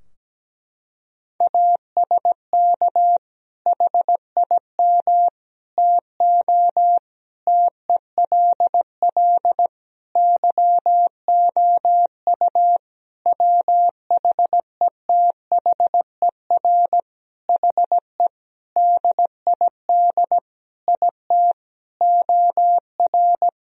Morse code, heard as a keyed tone, about 17 wpm, 700 Hz. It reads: ASK HIM TO TELL YOU WHETHER HE DID IT OR
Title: Sentences from Top 1000 Words - Review Part 3 - 17wpm